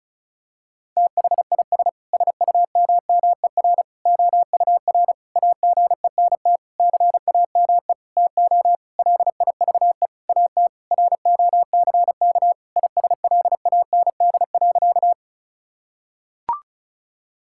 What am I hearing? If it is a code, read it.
Morse code, heard as a keyed tone, about 36 wpm, 700 Hz. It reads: THIS SUMMER OUR AGENT CAME TO LIVE AT ROCK ISLAND.
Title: Sentences from Top 1600 Words 35wpm